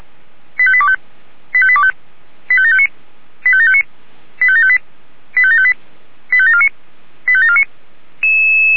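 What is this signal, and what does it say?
Five-tone selective call (ZVEI), heard as quick runs of five tones, tones of about 70 ms. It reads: 85825 85825 85759 85759 85758 85758 85749 85749
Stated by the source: ZVEI five-tone selective call received on SDR.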